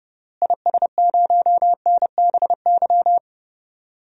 Morse code, heard as a keyed tone, about 30 wpm, 700 Hz. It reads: IS0NBY
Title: DXCC-100 Call Signs - 6 Characters - Repeated 3x - 30wpm